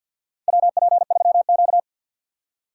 Morse code, heard as a keyed tone, about 51 wpm, 700 Hz. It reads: WP3Q
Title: USA Call Signs - 4 Characters - 3x Character Spacing - 50wpm